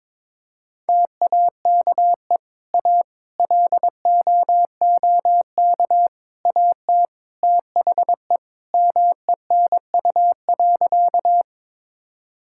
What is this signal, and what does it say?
Morse code, heard as a keyed tone, about 22 wpm, 700 Hz. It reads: TAKE A LOOK AT THE MENU.
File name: practice / 22wpm / Sentences with 4-Word Phrases - 22wpm